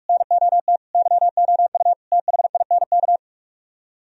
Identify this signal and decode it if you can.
Morse code, heard as a keyed tone, about 68 wpm, spaced as for 55 wpm, 700 Hz. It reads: NOT YOU THINK
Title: Speed-Racing - Sets of 4 Words 45wpm